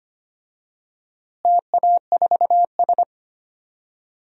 Morse code, heard as a keyed tone, about 25 wpm, 700 Hz. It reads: TA4S
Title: DXCC-100 Call Signs - 4 Characters - Repeated 3x - 25wpm